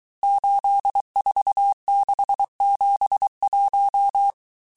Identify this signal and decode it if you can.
Morse code, heard as a keyed tone, about 23 wpm, 800 Hz. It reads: 84671
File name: cours-du-mercredi / audios / cours-855-texte-chiffre-vitesse-22-mpm